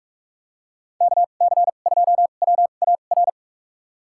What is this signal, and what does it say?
Morse code, heard as a keyed tone, about 46 wpm, spaced as for 32 wpm, 700 Hz. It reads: KC2WAR
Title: USA Call Signs - 6 Characters - 2x Character Spacing - 45wpm